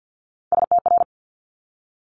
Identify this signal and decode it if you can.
Morse code, heard as a keyed tone, about 50 wpm, 700 Hz. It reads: STR
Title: Top Three Letter Combinations in Words 50wpm